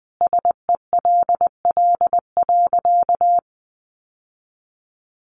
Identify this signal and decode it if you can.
Morse code, heard as a keyed tone, about 20 wpm, 700 Hz. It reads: SELL.
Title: Sentences from Top 1100 Words 20wpm